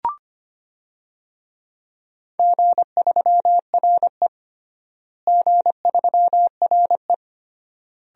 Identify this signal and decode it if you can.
Morse code, heard as a keyed tone, about 25 wpm, 700 Hz. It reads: G3RE G3RE
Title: DXCC-100 Call Signs - 4 Characters - Repeated 3x - 25wpm